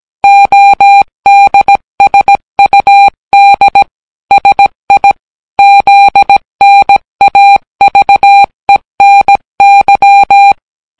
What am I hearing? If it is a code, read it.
Morse code, heard as a keyed tone, about 17 wpm, 800 Hz. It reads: ODSUD SI ZNAVENY